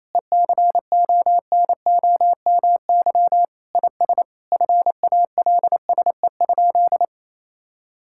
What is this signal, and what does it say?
Morse code, heard as a keyed tone, about 28 wpm, 700 Hz. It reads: ECONOMY IS FALSE?
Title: Sentences from Top 1600 Words 28wpm